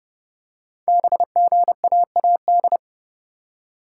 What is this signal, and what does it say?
Morse code, heard as a keyed tone, about 30 wpm, 700 Hz. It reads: BGAAD
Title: Code Groups - 5 Letters - 30wpm